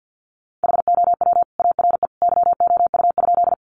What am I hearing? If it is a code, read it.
Morse code, heard as a keyed tone, about 50 wpm, 700 Hz. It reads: HOW ARE YOU?
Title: Sentences from Top 100 Words 50wpm